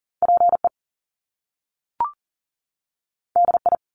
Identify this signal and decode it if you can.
Morse code, heard as a keyed tone, about 38 wpm, 700 Hz. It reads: PE DI
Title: Top Two Letter Combinations in Words 40wpm